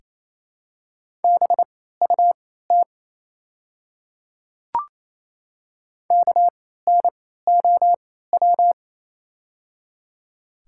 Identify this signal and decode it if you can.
Morse code, heard as a keyed tone, about 29 wpm, spaced as for 16 wpm, 700 Hz. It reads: BUT KNOW
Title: 3x Character Spacing - Top 300 Words - 28wpm